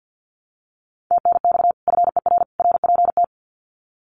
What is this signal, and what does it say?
Morse code, heard as a keyed tone, about 50 wpm, 700 Hz. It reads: TNX FER RPT
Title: Common QSO Phrases 50wpm